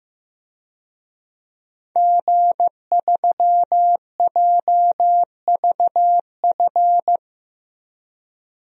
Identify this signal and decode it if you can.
Morse code, heard as a keyed tone, about 15 wpm, 700 Hz. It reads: G3JVF
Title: DXCC-100 Call Signs - 5 Characters - Repeated 2x - 15wpm